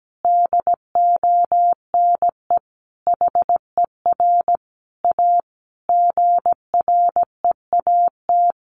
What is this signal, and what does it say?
Morse code, heard as a keyed tone, about 17 wpm, 700 Hz. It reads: DONE HER A GREAT
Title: Sentences from Top 900 Words 17wpm